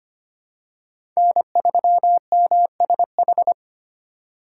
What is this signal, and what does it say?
Morse code, heard as a keyed tone, about 25 wpm, 700 Hz. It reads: N3MSH
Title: USA Call Signs - 5 Characters - 4x Character Spacing - 25wpm